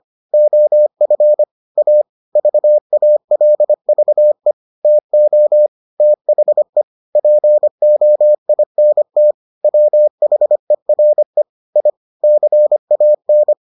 Morse code, heard as a keyed tone, about 25 wpm, 600 Hz. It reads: OF A VALVE TO THE POINT WHERE I CAN